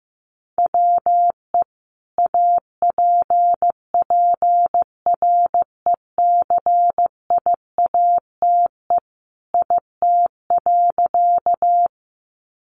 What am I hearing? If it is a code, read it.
Morse code, heard as a keyed tone, about 15 wpm, 700 Hz. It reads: WE APPRECIATE IT.
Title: Sentences from Top 1300 Words 15wpm